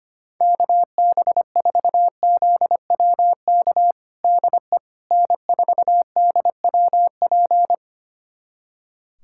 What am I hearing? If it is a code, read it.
Morse code, heard as a keyed tone, about 25 wpm, 700 Hz. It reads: KB4ZWK DE N4DWP
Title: USA Callsign DE Callsign 25wpm